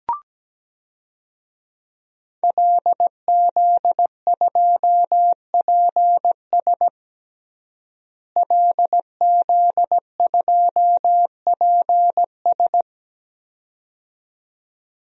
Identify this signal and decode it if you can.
Morse code, heard as a keyed tone, about 17 wpm, 700 Hz. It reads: LZ2PS LZ2PS
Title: DXCC-100 Call Signs - 5 Characters - Repeated 2x - 17wpm